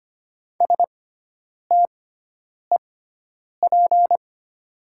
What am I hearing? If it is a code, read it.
Morse code, heard as a keyed tone, about 25 wpm, spaced as for 9 wpm, 700 Hz. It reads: STEP
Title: 6x Character Spacing - Top 500 Words - 25wpm